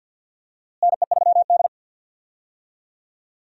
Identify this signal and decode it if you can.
Morse code, heard as a keyed tone, about 47 wpm, 700 Hz. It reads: NE3D
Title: USA Call Signs - 4 Characters - Repeated 2x - 50wpm